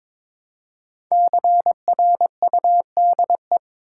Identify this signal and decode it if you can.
Morse code, heard as a keyed tone, about 22 wpm, 700 Hz. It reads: CRUDE
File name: 5 Letter Words - First 3 Letters Given - 22wpm